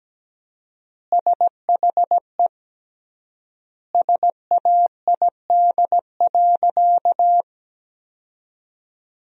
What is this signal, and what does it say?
Morse code, heard as a keyed tone, about 17 wpm, 700 Hz. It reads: SHE SAID.
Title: 3x Word-Spacing - Sentences from Top 300 Words - 17wpm